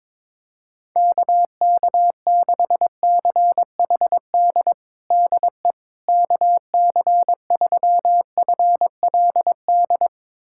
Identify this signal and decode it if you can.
Morse code, heard as a keyed tone, about 21 wpm, 700 Hz. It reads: KK6CHD DE KC3FLD
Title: USA Callsign DE Callsign - 6 Characters - 22wpm